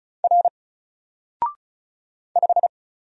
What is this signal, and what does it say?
Morse code, heard as a keyed tone, about 34 wpm, 700 Hz. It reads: R 5
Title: Single Letter-Number Alt 35wpm